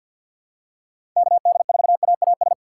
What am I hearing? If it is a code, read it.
Morse code, heard as a keyed tone, about 49 wpm, 700 Hz. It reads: KD4AAS
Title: USA Call Signs - 6 Characters - 4x Character Spacing - 50wpm